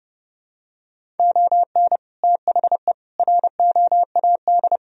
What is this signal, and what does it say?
Morse code, heard as a keyed tone, about 32 wpm, 700 Hz. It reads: ON THE ROAD
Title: Sentences from Top 1000 Words - Review Part 2 - 30wpm